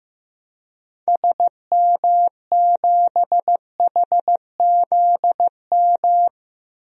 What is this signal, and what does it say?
Morse code, heard as a keyed tone, about 15 wpm, 700 Hz. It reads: SM7HZM
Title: DXCC-100 Call Signs - 6 Characters - Repeated 3x - 15wpm